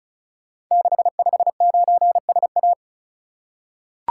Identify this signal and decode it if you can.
Morse code, heard as a keyed tone, about 36 wpm, 700 Hz. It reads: 659SA
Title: Code Groups - 5 Letter:Numbers - 35wpm